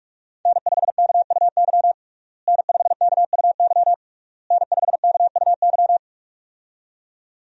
Morse code, heard as a keyed tone, about 46 wpm, 700 Hz. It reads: N5KUY N5KUY N5KUY
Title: USA Call Signs - 5 Characters - Repeated 3x - 45wpm